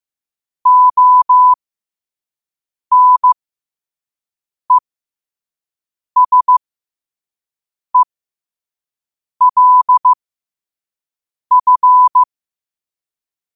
Morse code, heard as a keyed tone, about 15 wpm, spaced as for 5 wpm, 1000 Hz. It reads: ONESELF
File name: qotdfiles / 5wpm